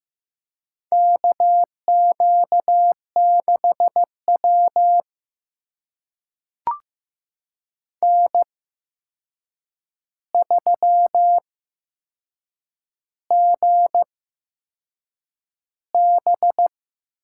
Morse code, heard as a keyed tone, about 15 wpm, spaced as for 6 wpm, 700 Hz. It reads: KQ6W N3GB
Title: USA Call Signs - 4 Characters - 8x Character Spacing - 15wpm